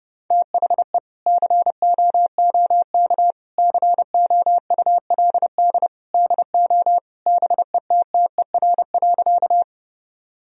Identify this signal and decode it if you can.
Morse code, heard as a keyed tone, about 30 wpm, 700 Hz. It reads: THE COOK COULD DO BETTER.
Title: Sentences from Top 1000 Words - Review Part 4 - 30wpm